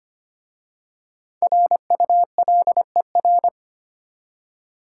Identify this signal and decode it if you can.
Morse code, heard as a keyed tone, about 25 wpm, 690 Hz. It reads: RULER
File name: practice / 25wpm / Easy Words with Hints - 25wpm